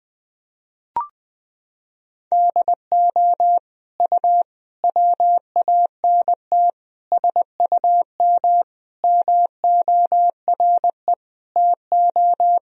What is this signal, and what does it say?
Morse code, heard as a keyed tone, about 21 wpm, 700 Hz. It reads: DO U WANT SUM MORE TO
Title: Sentences from Top 300 Words and Abbreviations 20wpm